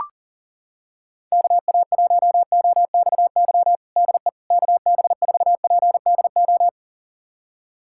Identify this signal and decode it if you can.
Morse code, heard as a keyed tone, about 40 wpm, 700 Hz. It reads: KA1OXY DE KB4PDO
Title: USA Callsign DE Callsign - 6 Characters - 40wpm